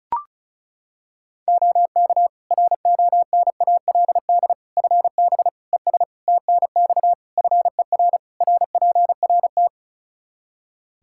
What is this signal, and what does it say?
Morse code, heard as a keyed tone, about 36 wpm, 700 Hz. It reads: OK RONALD FB ES TNX FER RPRT
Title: Practice QSO - Exchange 2 - Info - 35wpm